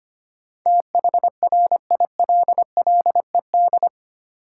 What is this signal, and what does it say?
Morse code, heard as a keyed tone, about 25 wpm, 690 Hz. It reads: THRILLED
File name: 8 Letter Words - First 3 Letters Given - 25wpm